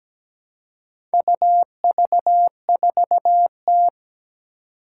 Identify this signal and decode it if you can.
Morse code, heard as a keyed tone, about 17 wpm, 700 Hz. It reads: UV4T
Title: Four Challenging Letter-Number Combinations 17wpm